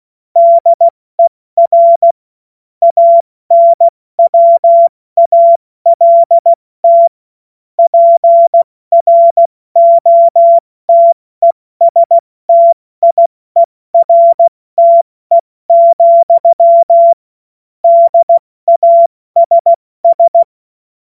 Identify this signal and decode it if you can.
Morse code, heard as a keyed tone, about 16 wpm, 680 Hz. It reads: DER ANWALT PROTESTIERTE, DASS
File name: Morsekurs / Witze / Witz 30. August 2022 70 F80